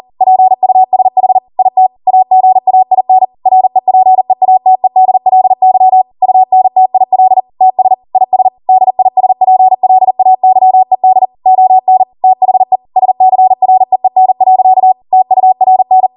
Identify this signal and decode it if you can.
Morse code, heard as a keyed tone, about 40 wpm, 750 Hz. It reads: PUSH IT AGAIN REPEATEDLY UNTIL TS IS DISPLAYED ON THE SCREEN. TURN
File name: Morse / 40WPM